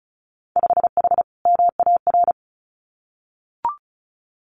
Morse code, heard as a keyed tone, about 34 wpm, 700 Hz. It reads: 5H MAR
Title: Winter Field Day Exchanges Encore 35wpm